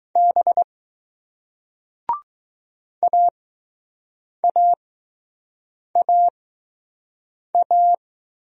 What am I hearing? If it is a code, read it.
Morse code, heard as a keyed tone, about 20 wpm, spaced as for 6 wpm, 700 Hz. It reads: B AAAA